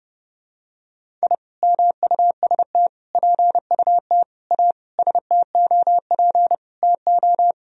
Morse code, heard as a keyed tone, about 30 wpm, 700 Hz. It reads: I MUST PUT A STOP TO
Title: Sentences from Top 1000 Words - Review Part 1 - 30wpm